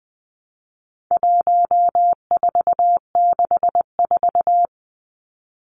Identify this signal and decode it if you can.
Morse code, heard as a keyed tone, about 20 wpm, 700 Hz. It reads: 1464
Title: Numbers - 4-Digit 20wpm